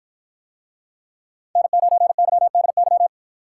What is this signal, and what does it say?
Morse code, heard as a keyed tone, about 53 wpm, 700 Hz. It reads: N9YDY